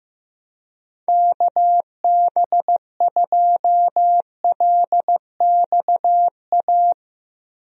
Morse code, heard as a keyed tone, about 15 wpm, 700 Hz. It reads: KB2LXA
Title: USA Call Signs - 6 Characters - Repeated 3x - 15wpm